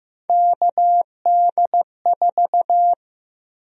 Morse code, heard as a keyed tone, about 15 wpm, 700 Hz. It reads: KD4
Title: USA Call Sign Prefixes - 15wpm